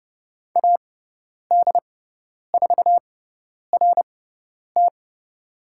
Morse code, heard as a keyed tone, about 30 wpm, spaced as for 10 wpm, 700 Hz. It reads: AD4RT
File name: Lesson 16g - Letter c - Call Signs 30wpm